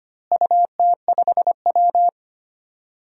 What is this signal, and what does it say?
Morse code, heard as a keyed tone, about 25 wpm, 700 Hz. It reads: UT5W
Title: DXCC-100 Call Signs - 4 Characters - Repeated 2x - 25wpm